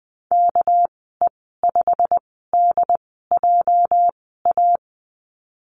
Morse code, heard as a keyed tone, about 20 wpm, spaced as for 14 wpm, 700 Hz. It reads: KE5DJA
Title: 213 Lesson 34g - Number 8 - Call Signs 20wpm